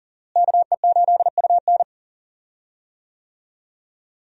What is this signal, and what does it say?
Morse code, heard as a keyed tone, about 40 wpm, 700 Hz. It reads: KE8UN
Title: USA Call Signs - 5 Characters - Repeated 2x - 40wpm